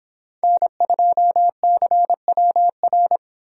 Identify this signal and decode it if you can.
Morse code, heard as a keyed tone, about 26 wpm, 700 Hz. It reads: N2CWR